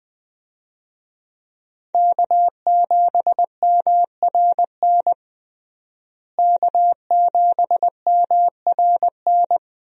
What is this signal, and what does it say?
Morse code, heard as a keyed tone, about 20 wpm, 700 Hz. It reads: K7MRN K7MRN
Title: USA Call Signs - 5 Characters - Repeated 3x - 20wpm